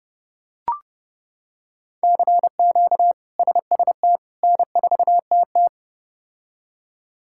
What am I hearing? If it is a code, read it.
Morse code, heard as a keyed tone, about 30 wpm, 700 Hz. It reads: CQ SST N4TT
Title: K1USN SST QSO Practice - 30wpm